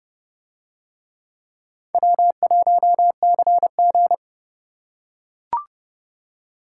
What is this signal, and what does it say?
Morse code, heard as a keyed tone, about 29 wpm, 700 Hz. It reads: W1CG